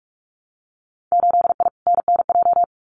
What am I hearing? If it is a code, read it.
Morse code, heard as a keyed tone, about 46 wpm, 700 Hz. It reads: 8I NNJ
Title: Winter Field Day Exchanges Encore 45wpm